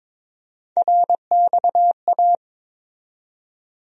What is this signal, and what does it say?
Morse code, heard as a keyed tone, about 22 wpm, 700 Hz. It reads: RXA
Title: USA Call Sign Postfixes - 3 Letters 22wpm